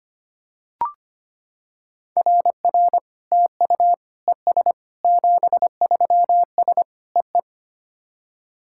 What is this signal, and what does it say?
Morse code, heard as a keyed tone, about 25 wpm, 700 Hz. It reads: RR TU ES 73S EE
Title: SOTA QSO Practice 25wpm